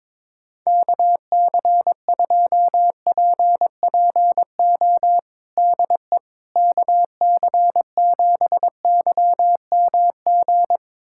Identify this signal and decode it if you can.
Morse code, heard as a keyed tone, about 22 wpm, 700 Hz. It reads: KC2PPO DE KC7YMG
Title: USA Callsign DE Callsign - 6 Characters - 22wpm